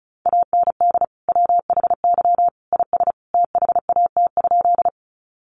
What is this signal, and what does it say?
Morse code, heard as a keyed tone, about 34 wpm, 700 Hz. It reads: AND WHY IS THAT?
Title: Sentences from Top 200 Words 35wpm